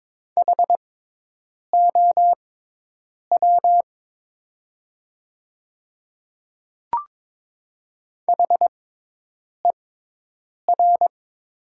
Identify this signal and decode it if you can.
Morse code, heard as a keyed tone, about 22 wpm, spaced as for 8 wpm, 700 Hz. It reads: HOW HER
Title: 6x Character Spacing - Top 200 Words - 22wpm